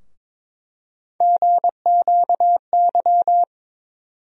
Morse code, heard as a keyed tone, about 22 wpm, 700 Hz. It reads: GQY
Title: USA Call Sign Postfixes - 3 Letters 22wpm